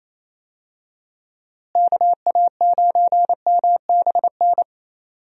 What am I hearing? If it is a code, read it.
Morse code, heard as a keyed tone, about 27 wpm, 700 Hz. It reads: KA9MBN
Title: USA Call Signs - 6 Characters - Repeated 2x - 28wpm